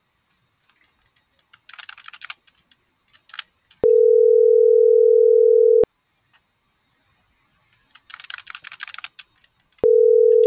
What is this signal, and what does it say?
Ringback tone (North America), 440 + 480 Hz.